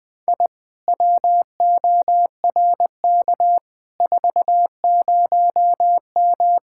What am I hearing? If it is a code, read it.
Morse code, heard as a keyed tone, about 20 wpm, 700 Hz. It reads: I WORK 40M
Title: Speed-Racing Common QSO Phrases - 17wpm